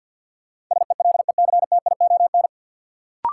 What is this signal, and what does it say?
Morse code, heard as a keyed tone, about 53 wpm, 700 Hz. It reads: SELECTION
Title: Sentences from Top 1400 Words - New Word Drill - 50wpm